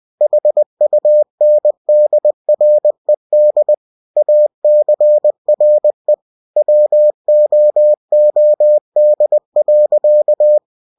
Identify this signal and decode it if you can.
Morse code, heard as a keyed tone, about 20 wpm, 600 Hz. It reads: HUNDRED ACRE WOOD.
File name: cw-books / winnie-the-pooh / winnie-20wpm-page-0028